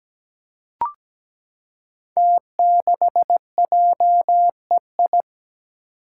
Morse code, heard as a keyed tone, about 17 wpm, 700 Hz. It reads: T6JEI
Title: Five Challenging Letter-Number Combinations 17wpm